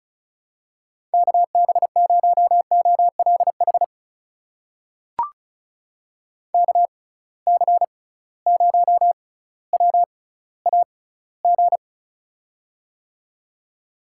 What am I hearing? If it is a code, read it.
Morse code, heard as a keyed tone, about 36 wpm, spaced as for 18 wpm, 700 Hz. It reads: KB0OLH KC0WAG
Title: USA Call Signs - 6x Character Spacing - 35wpm